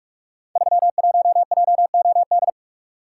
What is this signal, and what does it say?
Morse code, heard as a keyed tone, about 44 wpm, 700 Hz. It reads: 31JOD
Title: Code Groups - 5 Letter:Numbers - 45wpm